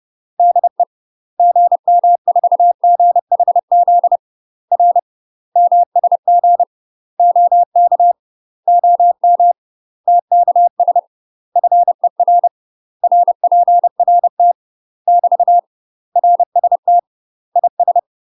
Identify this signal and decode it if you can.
Morse code, heard as a keyed tone, about 30 wpm, 700 Hz. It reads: DE GM4GHZ R MSG OK OM TKS FER RPRT <BT> RST IS
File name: UK MASTER QSOs TRAINING / 30WPM / UK MASTER QSOs 0028